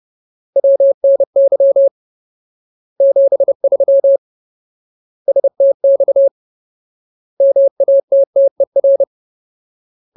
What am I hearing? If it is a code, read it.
Morse code, heard as a keyed tone, about 30 wpm, 550 Hz. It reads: WNY 73 STX MATTER
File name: Ultimate Word List / 30WPM Ultimate